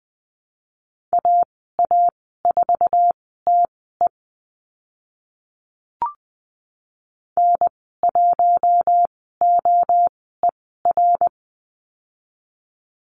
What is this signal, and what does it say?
Morse code, heard as a keyed tone, about 20 wpm, spaced as for 14 wpm, 700 Hz. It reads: AA4TE N1OER
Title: 055 Lesson 10g - Letter r - Call Signs 20wpm